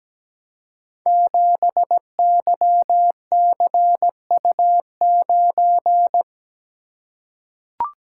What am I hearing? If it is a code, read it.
Morse code, heard as a keyed tone, about 17 wpm, 700 Hz. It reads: 7YCU9
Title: Code Groups - 5 Letter:Numbers - 17wpm